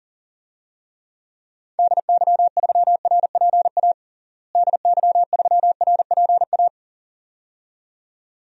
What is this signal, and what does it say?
Morse code, heard as a keyed tone, about 40 wpm, 700 Hz. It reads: DY3RPA DY3RPA
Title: DXCC-100 Call Signs - 6 Characters - Repeated 2x - 40wpm